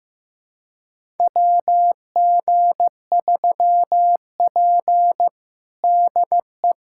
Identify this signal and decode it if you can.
Morse code, heard as a keyed tone, about 16 wpm, 700 Hz. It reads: WG3P DE
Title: USA Callsign DE Callsign - 4 Characters - 15wpm